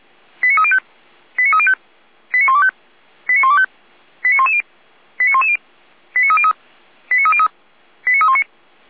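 Five-tone selective call (ZVEI), heard as quick runs of five tones, tones of about 70 ms. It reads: 89305 89305 89125 89125 89110 89110 89303 89303 89319
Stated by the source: ZVEI five-tone selective call received on an SDR.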